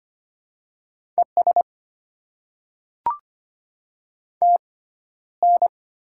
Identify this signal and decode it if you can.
Morse code, heard as a keyed tone, about 25 wpm, spaced as for 13 wpm, 700 Hz. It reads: ES TN
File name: Top 20 QSO Elements - 6x Character Spacing - 25wpm